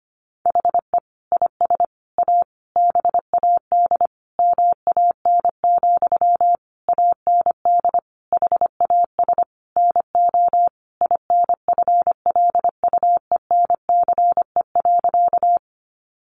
Morse code, heard as a keyed tone, about 25 wpm, 700 Hz. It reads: HE IS A BAD MAN, AND HAS NO INFLUENCE.
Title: Sentences from Top 1100 Words 25wpm